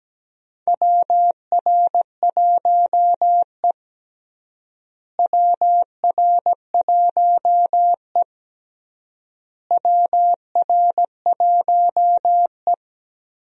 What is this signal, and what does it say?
Morse code, heard as a keyed tone, about 17 wpm, 700 Hz. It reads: WR1E WR1E WR1E
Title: USA Call Signs - 4 Characters - Repeated 3x - 17wpm